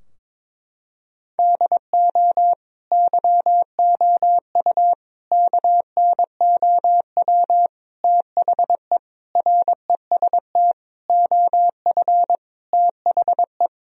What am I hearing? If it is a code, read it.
Morse code, heard as a keyed tone, about 21 wpm, 700 Hz. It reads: DO YOU KNOW THE REST OF THE
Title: Sentences from Top 1000 Words - Review Part 2 - 22wpm